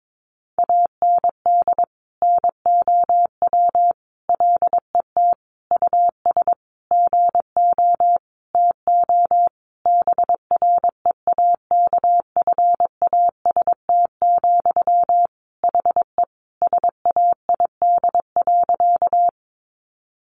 Morse code, heard as a keyed tone, about 22 wpm, 700 Hz. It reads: AND NOW LET US GO TO BREAKFAST, HE SAID.
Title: Sentences from Top 1500 Words 22wpm